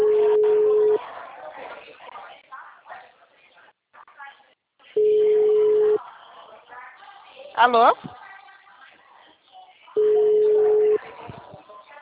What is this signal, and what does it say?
Ringback tone (Europe, 425 Hz), with speech.